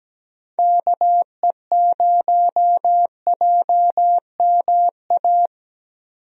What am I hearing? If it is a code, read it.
Morse code, heard as a keyed tone, about 17 wpm, 700 Hz. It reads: KE0JMA
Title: USA Call Signs - 6 Characters - 17wpm